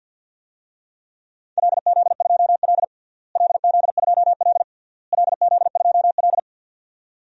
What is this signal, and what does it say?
Morse code, heard as a keyed tone, about 52 wpm, 700 Hz. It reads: LZ2L LZ2L LZ2L
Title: DXCC-100 Call Signs - 4 Characters - Repeated 3x - 50wpm